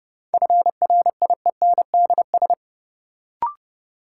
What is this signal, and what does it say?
Morse code, heard as a keyed tone, about 30 wpm, 700 Hz. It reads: FRIENDS
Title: Sentences from Top 1600 Words - New Word Drill Alt - 30wpm